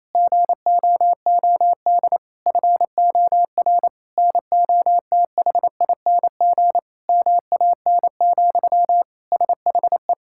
Morse code, heard as a keyed tone, about 28 wpm, 700 Hz. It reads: GOOD FOR NOTHING MAN, SHE
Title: Sentences from Top 1000 Words - Review Part 6 - 28wpm